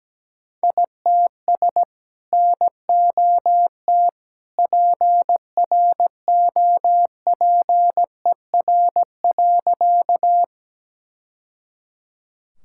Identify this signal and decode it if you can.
Morse code, heard as a keyed tone, about 17 wpm, 700 Hz. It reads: ITS NOT PROPER.
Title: Sentences from Top 1000 Words - Review Part 5 - 17wpm